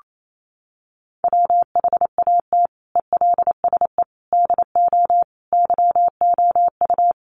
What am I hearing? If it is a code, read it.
Morse code, heard as a keyed tone, about 28 wpm, 700 Hz. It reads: WHAT ELSE DO YOU
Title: Sentences from Top 1100 Words 28wpm